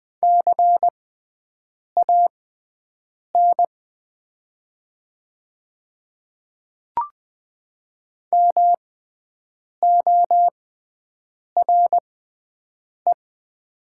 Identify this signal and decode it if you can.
Morse code, heard as a keyed tone, about 20 wpm, spaced as for 7 wpm, 700 Hz. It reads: CAN MORE
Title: 6x Character Spacing - Top 100 Words - 20wpm